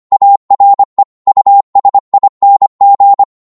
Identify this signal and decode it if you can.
Morse code, heard as a keyed tone, about 25 wpm, 800 Hz. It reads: ARE USING
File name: Morse / qso-25wpm